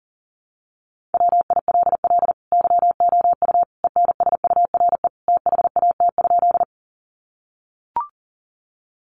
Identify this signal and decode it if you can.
Morse code, heard as a keyed tone, about 40 wpm, 700 Hz. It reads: WILL YOU ENSURE THAT?
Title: Sentences from Top 1100 Words 40wpm